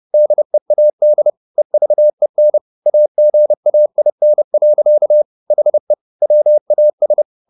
Morse code, heard as a keyed tone, about 30 wpm, 600 Hz. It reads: DEAD EVEN AGAIN. HE WAS